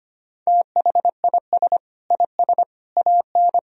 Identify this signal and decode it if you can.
Morse code, heard as a keyed tone, about 25 wpm, 700 Hz. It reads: THIS IS AN
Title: Sentences from Top 1000 Words - Review Part 2 - 25wpm